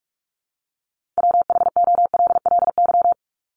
Morse code, heard as a keyed tone, about 44 wpm, 700 Hz. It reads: WHOLLY
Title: LY Suffix - Top 100 Words with Base Words 45wpm